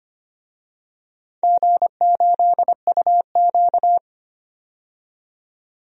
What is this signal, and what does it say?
Morse code, heard as a keyed tone, about 25 wpm, 700 Hz. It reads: G8UQ
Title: DXCC-100 Call Signs - 4 Characters - 25wpm